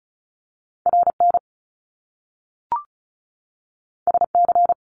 Morse code, heard as a keyed tone, about 36 wpm, 700 Hz. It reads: RN SC
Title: Top Two Letter Combinations in Words 35wpm